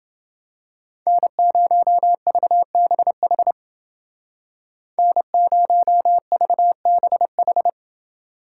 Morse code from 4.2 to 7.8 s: N0VBH

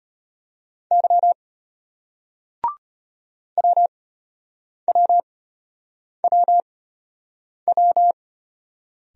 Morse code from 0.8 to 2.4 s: Y

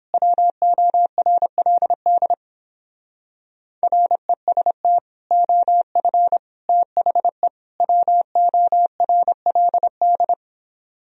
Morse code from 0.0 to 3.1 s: WORLD

